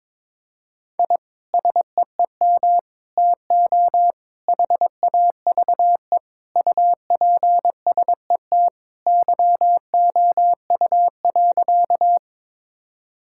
Morse code, heard as a keyed tone, about 22 wpm, 700 Hz. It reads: I SEEM TO HAVE UPSET YOU.
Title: Sentences from Top 1000 Words - Review Part 5 - 22wpm